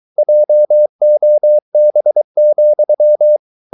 Morse code, heard as a keyed tone, about 23 wpm, 600 Hz. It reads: JOB,